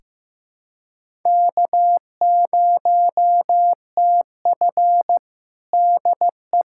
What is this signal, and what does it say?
Morse code, heard as a keyed tone, about 15 wpm, 700 Hz. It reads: K0TF DE
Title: USA Callsign DE Callsign - 4 Characters - 15wpm